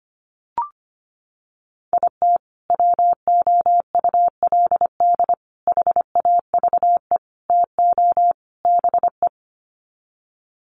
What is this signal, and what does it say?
Morse code, heard as a keyed tone, about 25 wpm, 700 Hz. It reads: IT WOULD HAVE TO BE
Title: Sets of 5 Words Encore 25wpm